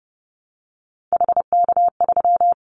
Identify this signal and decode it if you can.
Morse code, heard as a keyed tone, about 30 wpm, 700 Hz. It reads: HK3
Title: Callsign Prefixes for Top 100 DXCC - 30wpm